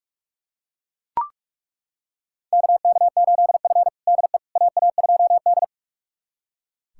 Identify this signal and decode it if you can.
Morse code, heard as a keyed tone, about 42 wpm, 690 Hz. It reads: KK8F DE AA2D
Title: USA Callsign DE Callsign - 4 Characters - 45wpm